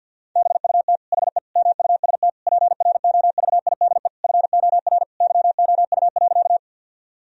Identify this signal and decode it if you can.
Morse code, heard as a keyed tone, about 50 wpm, 700 Hz. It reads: BUT HE MUST PROVIDE FOR YOU.
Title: Sentences from Top 1000 Words - Review Part 3 - 50wpm